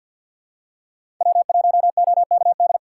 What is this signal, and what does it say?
Morse code, heard as a keyed tone, about 46 wpm, 700 Hz. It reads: W1OKD
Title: USA Call Signs - 4x Character Spacing - 50wpm